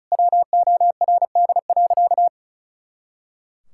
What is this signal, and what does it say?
Morse code, heard as a keyed tone, about 36 wpm, 700 Hz. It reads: WORD.
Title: Pangram Challenge v2 - 35wpm